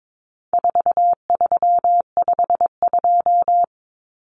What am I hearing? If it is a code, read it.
Morse code, heard as a keyed tone, about 22 wpm, 700 Hz. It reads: 4352
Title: Common Cut Numbers - 4-Digit 22wpm